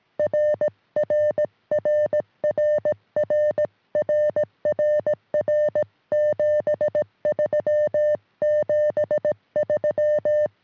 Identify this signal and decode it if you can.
Morse code, heard as a keyed tone, about 17 wpm, 600 Hz. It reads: RRRRRRRR 73 73
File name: sm7lcb / 2020 ew1ew 23cm